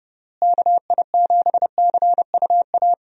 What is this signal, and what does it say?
Morse code, heard as a keyed tone, about 30 wpm, 700 Hz. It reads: KI7CUA
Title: USA Call Signs - 6 Characters - Repeated 2x - 30wpm